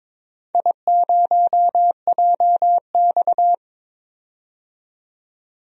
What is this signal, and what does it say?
Morse code, heard as a keyed tone, about 22 wpm, 700 Hz. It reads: I0JX
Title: DXCC-100 Call Signs - 4 Characters - 22wpm